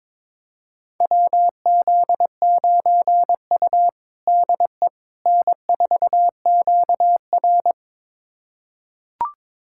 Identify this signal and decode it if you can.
Morse code, heard as a keyed tone, about 22 wpm, 700 Hz. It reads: WZ9U DE N4QR
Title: USA Callsign DE Callsign - 4 Characters - 22wpm